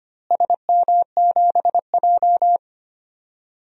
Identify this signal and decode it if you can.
Morse code, heard as a keyed tone, about 25 wpm, 700 Hz. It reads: SM7J
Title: DXCC-100 Call Signs - 4 Characters - Repeated 2x - 25wpm